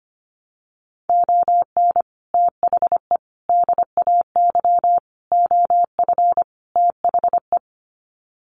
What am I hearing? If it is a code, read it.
Morse code, heard as a keyed tone, about 25 wpm, 700 Hz. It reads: ON THE DAY OF THE
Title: Sets of 5 Words Encore 25wpm